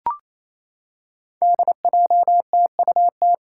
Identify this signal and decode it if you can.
Morse code, heard as a keyed tone, about 28 wpm, 700 Hz. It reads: DJTUT
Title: Five Challenging Letter-Number Combinations 28wpm